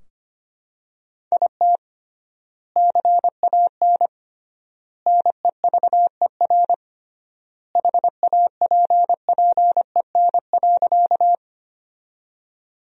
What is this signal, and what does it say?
Morse code, heard as a keyed tone, about 25 wpm, 700 Hz. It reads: IT CAN NEVER HAPPEN.
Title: 3x Word-Spacing - Sentences from Top 400 Words - 25wpm